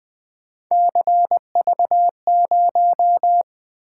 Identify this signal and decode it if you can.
Morse code, heard as a keyed tone, about 20 wpm, 700 Hz. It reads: CV0
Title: Three Letter-Number Combinations - 20wpm